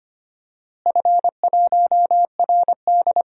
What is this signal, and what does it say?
Morse code, heard as a keyed tone, about 25 wpm, 700 Hz. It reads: F1RD
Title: DXCC-100 Call Signs - 4 Characters - Repeated 3x - 25wpm